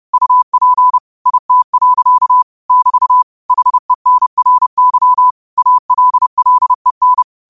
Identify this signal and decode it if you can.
Morse code, heard as a keyed tone, about 30 wpm, 1000 Hz. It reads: UP IT. <BT> HENRY ALLEN
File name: qotdfiles / 30wpm